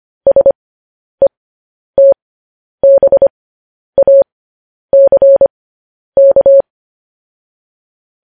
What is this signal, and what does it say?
Morse code, heard as a keyed tone, about 25 wpm, spaced as for 10 wpm, 550 Hz. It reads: SETBACK
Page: news-headlines-in-morse-code-at-10-wpm